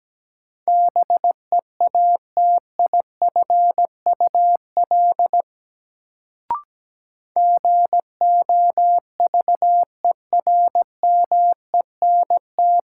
Morse code, heard as a keyed tone, about 17 wpm, 700 Hz. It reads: BEATIFUL GOVERMENT
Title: Sentences from Top 1300 Words - New Word Drill - 17wpm